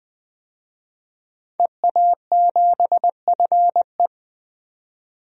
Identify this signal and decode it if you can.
Morse code, heard as a keyed tone, about 20 wpm, 700 Hz. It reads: EA7FE
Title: DXCC-100 Call Signs - 5 Characters - 20wpm